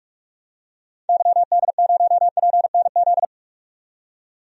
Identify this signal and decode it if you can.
Morse code, heard as a keyed tone, about 46 wpm, 700 Hz. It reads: YD0PNZ
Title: DXCC-100 Call Signs - 6 Characters - 45wpm